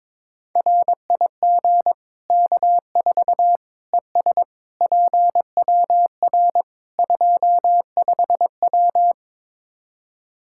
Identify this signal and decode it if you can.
Morse code, heard as a keyed tone, about 22 wpm, 700 Hz. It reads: RIG K4 ES PWR 25W
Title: Practice QSO - Exchange 2 - Info - 22wpm